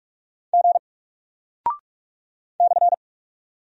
Morse code, heard as a keyed tone, about 48 wpm, 700 Hz. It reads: G /
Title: Single Character (Letter:Number:Punctuation:Prosign) Alt 45wpm